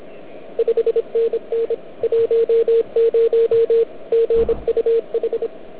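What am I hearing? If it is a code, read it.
Morse code, heard as a keyed tone, about 24 wpm, 440 Hz. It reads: 5NN 10 GUH
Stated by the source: Morse from the OK QRP 2013 contest, run at 1 watt on a 12.3V lithium battery, Liberci, with OK1MKX also at 1W.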